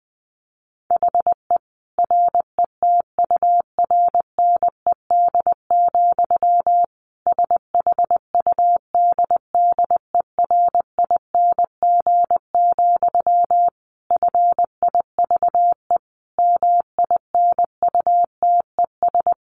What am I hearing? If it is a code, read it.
Morse code, heard as a keyed tone, about 20 wpm, 700 Hz. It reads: HE RETURNED, SHUDDERING, FIVE MINUTES
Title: A Sound Of Thunder 20wpm